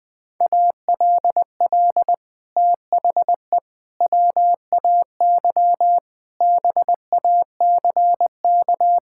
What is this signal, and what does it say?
Morse code, heard as a keyed tone, about 20 wpm, 700 Hz. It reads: ALL THE WAY BACK
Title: Sets of 6 Words - 20wpm